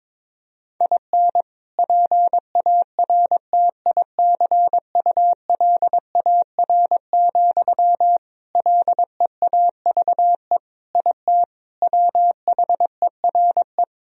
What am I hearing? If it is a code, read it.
Morse code, heard as a keyed tone, about 22 wpm, 700 Hz. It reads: IN PARTICULAR, LEAVE IT WHERE
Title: Sentences from Top 1000 Words - Review Part 4 - 22wpm